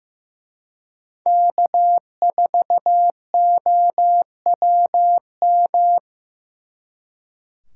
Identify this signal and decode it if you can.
Morse code, heard as a keyed tone, about 19 wpm, 700 Hz. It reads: K4OWM